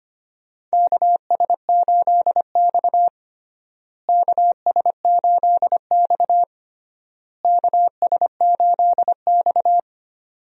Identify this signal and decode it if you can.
Morse code, heard as a keyed tone, about 25 wpm, 700 Hz. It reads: KS8X KS8X KS8X
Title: USA Call Signs - 4 Characters - Repeated 3x - 25wpm